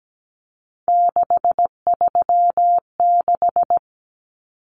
Morse code, heard as a keyed tone, about 17 wpm, 700 Hz. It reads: 636
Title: Numbers - 3-Digit 17wpm